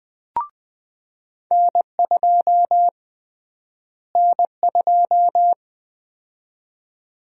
Morse code, heard as a keyed tone, about 20 wpm, 700 Hz. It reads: N2 N2
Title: USA Call Sign Prefixes - Repeated 2x - 20wpm